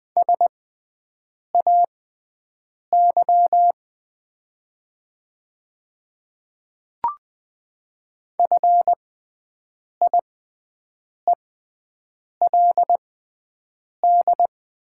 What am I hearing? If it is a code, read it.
Morse code, heard as a keyed tone, about 20 wpm, spaced as for 7 wpm, 700 Hz. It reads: SAY FIELD
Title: 6x Character Spacing - Top 500 Words - 20wpm